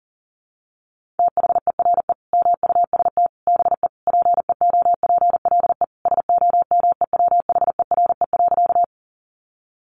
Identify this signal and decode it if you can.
Morse code, heard as a keyed tone, about 40 wpm, 700 Hz. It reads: THERE MUST BE PEOPLE SOMEWHERE.
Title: Sentences from Top 1400 Words 40wpm